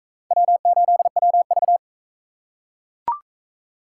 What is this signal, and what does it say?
Morse code, heard as a keyed tone, about 42 wpm, 700 Hz. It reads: W8WV